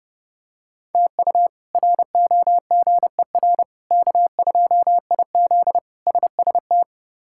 Morse code, heard as a keyed tone, about 30 wpm, 700 Hz. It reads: TU ROGER K2IZ SST
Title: K1USN SST QSO Practice - 30wpm